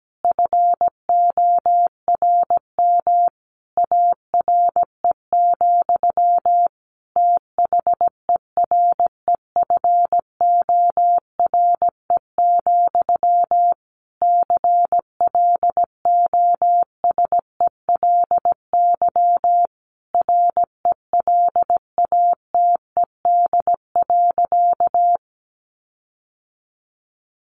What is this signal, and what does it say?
Morse code, heard as a keyed tone, about 17 wpm, 700 Hz. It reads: FORM ARE, THEREFORE, CLOSELY RELATED.
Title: Sentences from Top 1400 Words 17wpm